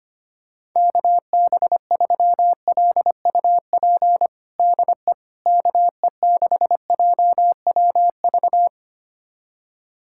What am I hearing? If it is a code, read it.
Morse code, heard as a keyed tone, about 25 wpm, 700 Hz. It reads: KB3LUP DE KE6JWV
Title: USA Callsign DE Callsign 25wpm